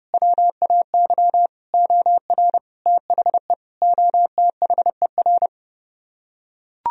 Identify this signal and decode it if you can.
Morse code, heard as a keyed tone, about 31 wpm, 700 Hz. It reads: WAY OR THE OTHER